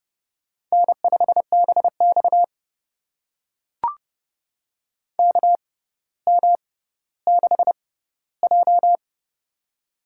Morse code, heard as a keyed tone, about 30 wpm, spaced as for 15 wpm, 700 Hz. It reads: N5BX KM6J
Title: USA Call Signs - 4 Characters - 6x Character Spacing - 30wpm